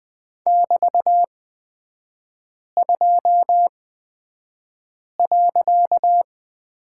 Morse code, heard as a keyed tone, about 20 wpm, spaced as for 5 wpm, 700 Hz. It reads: <BT>2.
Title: 255 Lesson 41e - Period - Three Characters 20wpm